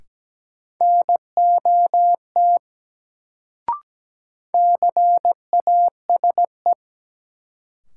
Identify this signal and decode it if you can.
Morse code, heard as a keyed tone, about 17 wpm, 700 Hz. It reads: NOT CASE
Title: Sets of 2 Words - New Word Drill - 17wpm